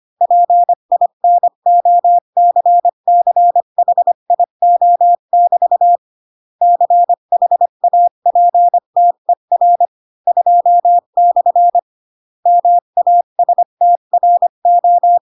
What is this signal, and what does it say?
Morse code, heard as a keyed tone, about 25 wpm, 700 Hz. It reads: PINOCCHIO<BT> CHAPTER 2/ MASTRO